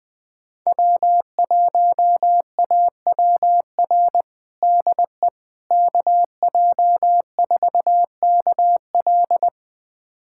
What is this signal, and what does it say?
Morse code, heard as a keyed tone, about 21 wpm, 700 Hz. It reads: W1AWR DE KJ4KL
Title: USA Callsign DE Callsign - 5 Characters - 20wpm